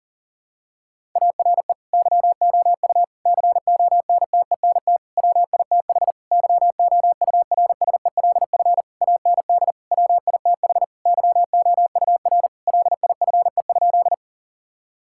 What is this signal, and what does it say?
Morse code, heard as a keyed tone, about 37 wpm, 700 Hz. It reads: ARE YOU CONTENT WITH YOURSELF AND WITH YOUR LIFE?
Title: Sentences from Top 1000 Words - Review Part 6 - 40wpm